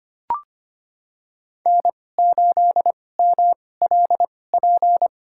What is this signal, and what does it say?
Morse code, heard as a keyed tone, about 25 wpm, spaced as for 18 wpm, 700 Hz. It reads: N8MLP
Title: USA Call Signs - 5 Characters - 2x Character Spacing - 25wpm